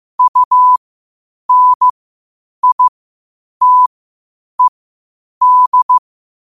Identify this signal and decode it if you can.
Morse code, heard as a keyed tone, about 15 wpm, spaced as for 8 wpm, 1000 Hz. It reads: UNITED
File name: qotdfiles / 8wpm